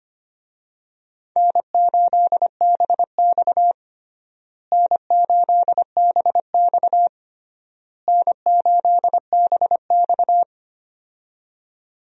Morse code, heard as a keyed tone, about 25 wpm, 700 Hz. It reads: N8BX N8BX N8BX
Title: USA Call Signs - 4 Characters - Repeated 3x - 25wpm